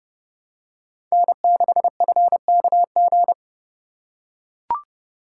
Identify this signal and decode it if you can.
Morse code, heard as a keyed tone, about 29 wpm, 700 Hz. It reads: N6FKG